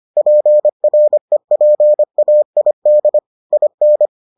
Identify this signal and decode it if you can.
Morse code, heard as a keyed tone, about 25 wpm, 600 Hz. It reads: PREPAID IN